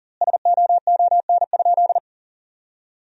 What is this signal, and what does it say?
Morse code, heard as a keyed tone, about 40 wpm, 700 Hz. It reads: SOON?
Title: 4x Word-Spacing - Sentences from Top 400 Words - 40wpm